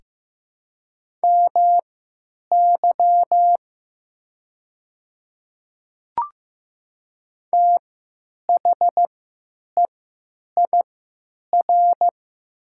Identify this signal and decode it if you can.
Morse code, heard as a keyed tone, about 15 wpm, spaced as for 9 wpm, 700 Hz. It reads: MY THEIR
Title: 3x Character Spacing - Top 100 Words - 15wpm